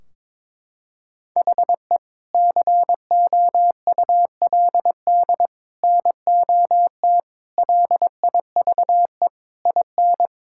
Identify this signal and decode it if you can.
Morse code, heard as a keyed tone, about 22 wpm, 700 Hz. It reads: HE COULD NOT LIVE IN
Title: Sentences from Top 1600 Words 22wpm